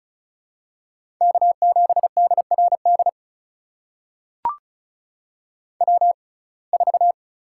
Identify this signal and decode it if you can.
Morse code, heard as a keyed tone, about 36 wpm, spaced as for 25 wpm, 700 Hz. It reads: K7DRD W4
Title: USA Call Signs - 5 Characters - 6x Character Spacing - 35wpm